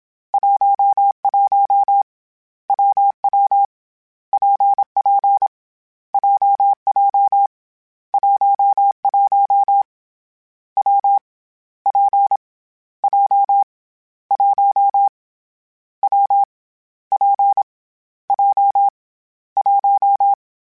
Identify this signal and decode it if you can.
Morse code, heard as a keyed tone, about 27 wpm, 800 Hz. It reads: 11 WW PP JJ 11 W P J 1 W P J 1
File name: Difficult Combos / Difficult Combos-mp3s / WPJ WPJ1 @25